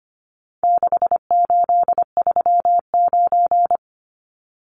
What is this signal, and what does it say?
Morse code, heard as a keyed tone, about 25 wpm, 700 Hz. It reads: 6839
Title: Numbers - 4-Digit 25wpm